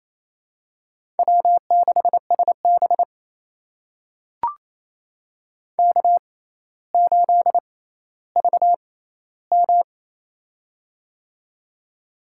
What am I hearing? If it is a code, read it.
Morse code, heard as a keyed tone, about 28 wpm, spaced as for 14 wpm, 700 Hz. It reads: W6SB K8VM
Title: USA Call Signs - 4 Characters - 6x Character Spacing - 28wpm